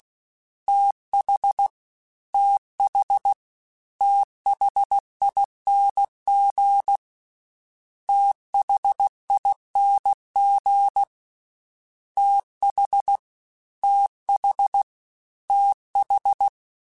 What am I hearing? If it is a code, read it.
Morse code, heard as a keyed tone, about 16 wpm, 800 Hz. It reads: TH TH THING THING TH TH TH
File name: Combinations / Combinations-mp3s / Combinations@15